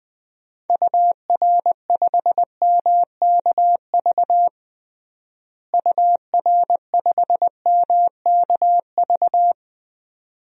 Morse code, heard as a keyed tone, about 20 wpm, 700 Hz. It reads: UR5MKV UR5MKV
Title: DXCC-100 Call Signs - 6 Characters - Repeated 2x - 20wpm